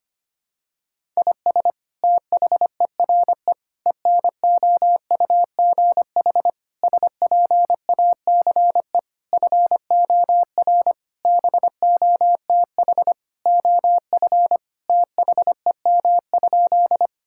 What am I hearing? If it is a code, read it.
Morse code, heard as a keyed tone, about 25 wpm, 700 Hz. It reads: IS THERE ENOUGH SPACE FOR BOTH OF THEM?
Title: Sentences from Top 1000 Words - Review Part 5 - 25wpm